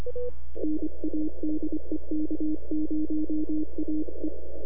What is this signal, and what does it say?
Morse code, heard as a keyed tone, about 25 wpm, 310 Hz. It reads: NA DE K0AE
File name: NAQPCW-8-2009 / W1UJ NAQP CW 09 2300Z